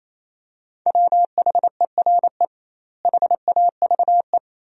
Morse code, heard as a keyed tone, about 28 wpm, 700 Hz. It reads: WHERE HAVE
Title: 2x Word-Spacing - Sentences from Top 200 Words - 28wpm